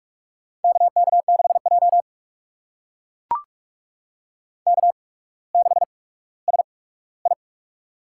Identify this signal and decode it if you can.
Morse code, heard as a keyed tone, about 44 wpm, spaced as for 19 wpm, 700 Hz. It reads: KK6J K6SI
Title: USA Call Signs - 4 Characters - 8x Character Spacing - 45wpm